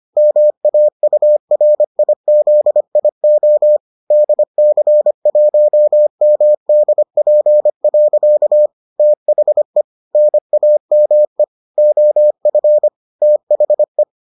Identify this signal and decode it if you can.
Morse code, heard as a keyed tone, about 25 wpm, 600 Hz. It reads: MAURIZIO DC1MDP. THE NAME OF THE